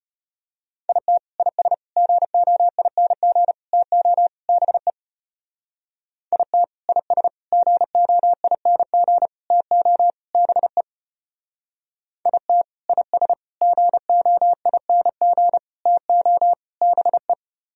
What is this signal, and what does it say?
Morse code, heard as a keyed tone, about 33 wpm, 700 Hz. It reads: IT IS GOING TO BE IT IS GOING TO BE IT IS GOING TO BE